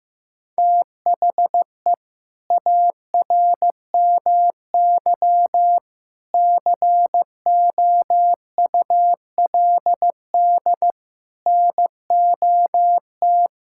Morse code, heard as a keyed tone, about 16 wpm, 700 Hz. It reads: THE ARMY COULD NOT
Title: Sentences from Top 1000 Words - Review Part 5 - 15wpm